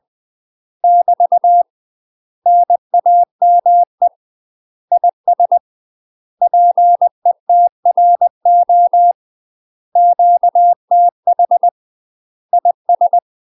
Morse code, heard as a keyed tone, about 20 wpm, 700 Hz. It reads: <BT> NAME IS PETRO QTH IS
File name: UK MASTER QSOs TRAINING / 20WPM / UK MASTER QSOs 0070